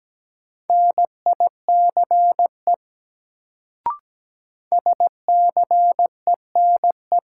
Morse code, heard as a keyed tone, about 17 wpm, 700 Hz. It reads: NICE SCENE
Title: Sentences from Top 1100 Words - New Word Drill Alt - 17wpm